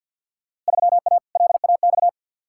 Morse code, heard as a keyed tone, about 51 wpm, 700 Hz. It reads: 3A LAX
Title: Field Day Exchanges - 50wpm